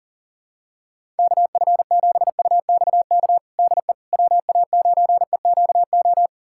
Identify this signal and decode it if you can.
Morse code, heard as a keyed tone, about 40 wpm, 700 Hz. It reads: KF7UXK DE WA9EQO
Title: USA Callsign DE Callsign 40wpm